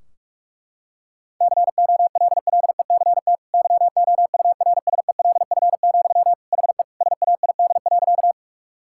Morse code, heard as a keyed tone, about 44 wpm, 700 Hz. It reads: COLLECT YOURSELF, HE SAID.
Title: Sentences from Top 1600 Words 45wpm